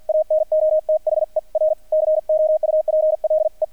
Morse code, heard as a keyed tone, about 48 wpm, 620 Hz, with radio noise. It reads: ANOTHER COUPLE
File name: archived / CRSnet